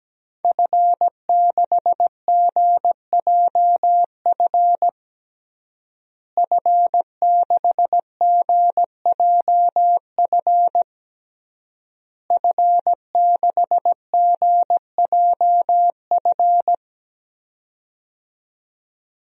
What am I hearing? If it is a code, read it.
Morse code, heard as a keyed tone, about 17 wpm, 700 Hz. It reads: F6GJF F6GJF F6GJF
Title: DXCC-100 Call Signs - 5 Characters - Repeated 3x - 17wpm